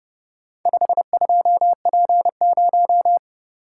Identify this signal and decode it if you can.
Morse code, heard as a keyed tone, about 29 wpm, 700 Hz. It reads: 52P0